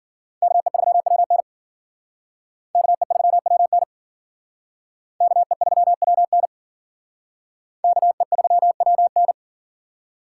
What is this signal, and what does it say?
Morse code, heard as a keyed tone, about 50 wpm, 700 Hz. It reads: KE3WN KE3WN KE3WN KE3WN
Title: Speed Racing - USA 2x2 Call Signs 40wpm